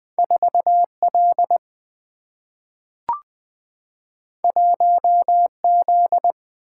Morse code, heard as a keyed tone, about 20 wpm, 700 Hz. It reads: 4L 1Z
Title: Two Letter-Number Combinations - 20wpm